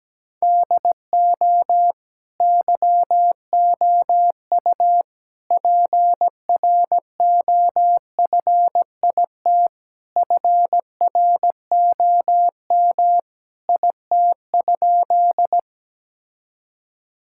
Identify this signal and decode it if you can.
Morse code, heard as a keyed tone, about 17 wpm, 700 Hz. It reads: DO YOU PROFIT FROM IT?
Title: Sentences from Top 1000 Words - Review Part 5 - 17wpm